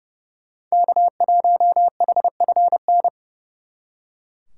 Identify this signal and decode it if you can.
Morse code, heard as a keyed tone, about 36 wpm, 700 Hz. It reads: K1HFN
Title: Speed Racing - USA 1x3 Call Signs 30wpm